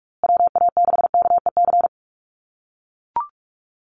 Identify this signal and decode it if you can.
Morse code, heard as a keyed tone, about 44 wpm, 690 Hz. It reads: WA6KEC
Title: USA Call Signs Repeated 2x 45wpm